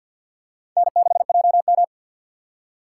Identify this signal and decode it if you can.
Morse code, heard as a keyed tone, about 49 wpm, 700 Hz. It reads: N6JM
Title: USA Call Signs - 2x Character Spacing - 50wpm